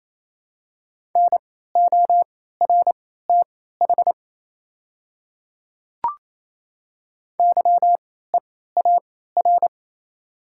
Morse code, heard as a keyed tone, about 28 wpm, spaced as for 16 wpm, 700 Hz. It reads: NORTH YEAR
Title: 3x Character Spacing - Top 300 Words - 28wpm